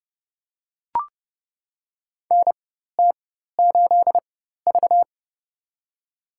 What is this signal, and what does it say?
Morse code, heard as a keyed tone, about 30 wpm, spaced as for 14 wpm, 700 Hz. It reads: NT8V
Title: USA Call Signs - 4 Characters - 4x Character Spacing - 30wpm